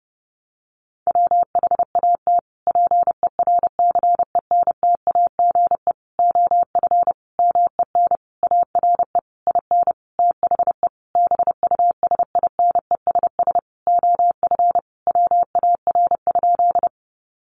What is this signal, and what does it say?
Morse code, heard as a keyed tone, about 30 wpm, 700 Hz. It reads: WHAT PERCENTAGE OF MEN ARE IN THE BUSINESS OF WAR?
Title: Sentences from Top 1300 Words 30wpm